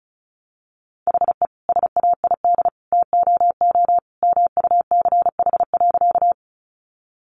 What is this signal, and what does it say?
Morse code, heard as a keyed tone, about 34 wpm, 700 Hz. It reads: HE SAID TOO MUCH.
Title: Sentences from Top 200 Words 35wpm